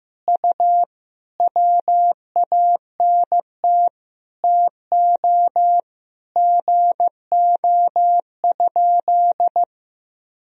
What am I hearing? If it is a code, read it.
Morse code, heard as a keyed tone, about 15 wpm, 700 Hz. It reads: U WANT TO GO?
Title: Sentences from Top 300 Words and Abbreviations 15wpm